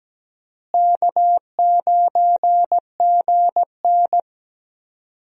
Morse code, heard as a keyed tone, about 17 wpm, 700 Hz. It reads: K9GN
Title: USA Call Signs - 4 Characters - Repeated 3x - 17wpm